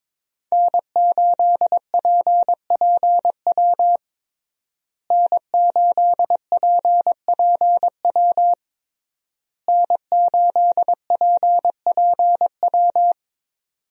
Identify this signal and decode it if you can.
Morse code, heard as a keyed tone, about 22 wpm, 700 Hz. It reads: N8PPW N8PPW N8PPW
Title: USA Call Signs - 5 Characters - Repeated 3x - 22wpm